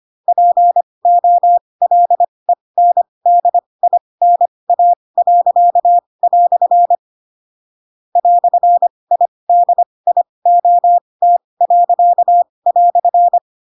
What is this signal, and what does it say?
Morse code, heard as a keyed tone, about 25 wpm, spaced as for 20 wpm, 700 Hz. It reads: POLENDINA." "IDIOT."
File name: Pinocchio / Pinocchio ch2 15 25 ik1hge